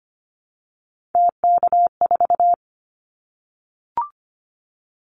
Morse code, heard as a keyed tone, about 25 wpm, 700 Hz. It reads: TK4
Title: Callsign Prefixes for Top 100 DXCC - 25wpm